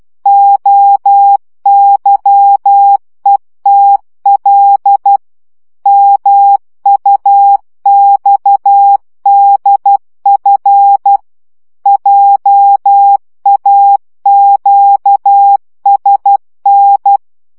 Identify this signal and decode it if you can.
Morse code, heard as a keyed tone, about 12 wpm, 800 Hz. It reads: OYETL MUXDF JAQSN
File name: morze vizsga 2011-3 10